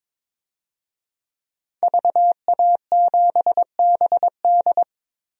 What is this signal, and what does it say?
Morse code, heard as a keyed tone, about 22 wpm, 700 Hz. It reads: VA7BD
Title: DXCC-100 Call Signs - 5 Characters - Repeated 2x - 22wpm